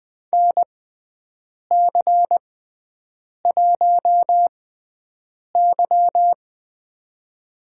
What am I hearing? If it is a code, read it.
Morse code, heard as a keyed tone, about 20 wpm, spaced as for 7 wpm, 700 Hz. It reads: NC1Y
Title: USA Call Signs - 4 Characters - 6x Character Spacing - 20wpm